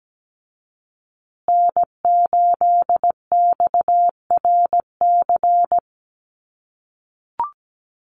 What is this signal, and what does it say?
Morse code, heard as a keyed tone, about 17 wpm, 700 Hz. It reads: N8XRC
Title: USA Call Signs Repeated 3x 17wpm